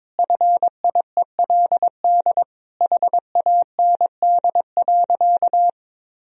Morse code, heard as a keyed tone, about 22 wpm, 700 Hz. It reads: FIELD HAND.
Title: Sentences from Top 1000 Words - Review Part 2 - 22wpm